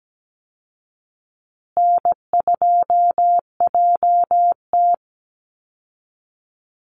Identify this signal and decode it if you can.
Morse code, heard as a keyed tone, about 17 wpm, 700 Hz. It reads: N2JT
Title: Easy Call Signs Repeated 2x - 17wpm